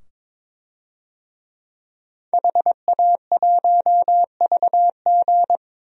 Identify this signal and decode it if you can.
Morse code, heard as a keyed tone, about 22 wpm, 700 Hz. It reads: HA1VG
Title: DXCC-100 Call Signs - 5 Characters - 22wpm